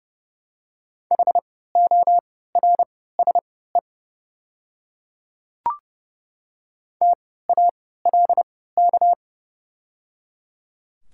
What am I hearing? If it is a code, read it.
Morse code, heard as a keyed tone, about 30 wpm, spaced as for 17 wpm, 700 Hz. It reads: HORSE TALK
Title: 3x Character Spacing - Top 500 Words - 30wpm